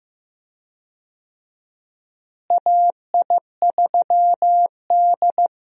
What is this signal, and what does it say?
Morse code, heard as a keyed tone, about 15 wpm, 700 Hz. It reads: AI3D
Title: USA Call Signs - 4 Characters - Repeated 3x - 15wpm